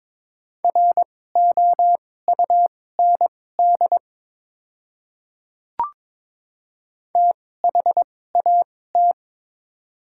Morse code, heard as a keyed tone, about 22 wpm, spaced as for 16 wpm, 700 Hz. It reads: ROUND THAT
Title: 2x Character Spacing - Top 200 Words - 22wpm